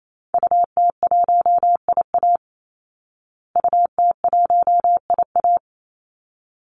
Morse code, heard as a keyed tone, about 28 wpm, 700 Hz. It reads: UT1IA UT1IA
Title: Call Signs Repeated 3x - 28wpm